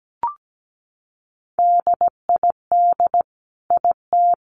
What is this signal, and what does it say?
Morse code, heard as a keyed tone, about 17 wpm, 700 Hz. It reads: DID IT
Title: Sentences from Top 1200 Words 17wpm